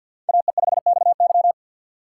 Morse code, heard as a keyed tone, about 53 wpm, 700 Hz. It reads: AE5XY
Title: USA Call Signs - 6x Character Spacing - 50wpm